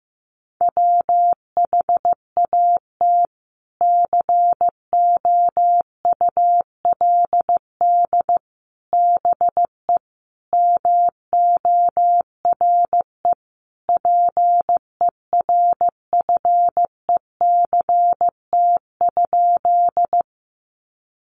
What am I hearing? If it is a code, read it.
Morse code, heard as a keyed tone, about 15 wpm, 700 Hz. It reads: WHAT COULD BE MORE PERFECT?
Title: Sentences from Top 1000 Words 15wpm